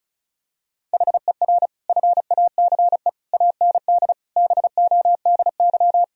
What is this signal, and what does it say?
Morse code, heard as a keyed tone, about 36 wpm, 700 Hz. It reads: HER FACE AND BODY
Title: Sentences from Top 1000 Words - Review Part 2 - 35wpm